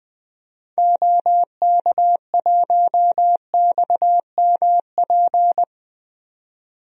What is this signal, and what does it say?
Morse code, heard as a keyed tone, about 20 wpm, 700 Hz. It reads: OK1XMP
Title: DXCC-100 Call Signs - 6 Characters - 20wpm